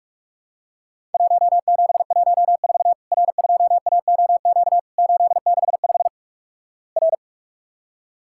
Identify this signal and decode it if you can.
Morse code, heard as a keyed tone, about 48 wpm, 700 Hz. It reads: 1714 R2AOQ 865 R
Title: Worked All Europe DX Contest - QTC Exchange - 45wpm